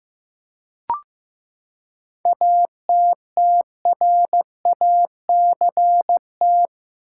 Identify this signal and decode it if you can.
Morse code, heard as a keyed tone, about 15 wpm, 700 Hz. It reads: ATTRACT
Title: Seven Letter Words - 15wpm